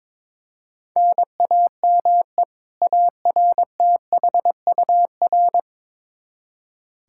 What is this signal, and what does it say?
Morse code, heard as a keyed tone, about 22 wpm, 700 Hz. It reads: NAME ARTHUR
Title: Practice QSO - Exchange 1 - Beginning - 22wpm